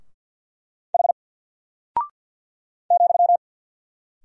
Morse code, heard as a keyed tone, about 48 wpm, 700 Hz. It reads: H ,
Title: Single Character (Letter:Number:Punctuation:Prosign) Alt 50wpm